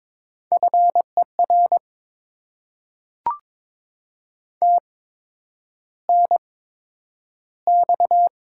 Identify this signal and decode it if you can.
Morse code, heard as a keyed tone, about 22 wpm, spaced as for 9 wpm, 700 Hz. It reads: FER TNX
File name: Top 20 QSO Elements - 8x Character Spacing - 22wpm